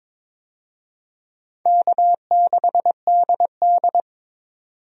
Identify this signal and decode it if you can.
Morse code, heard as a keyed tone, about 22 wpm, 700 Hz. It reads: K6DD
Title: USA Call Signs - 4 Characters - 22wpm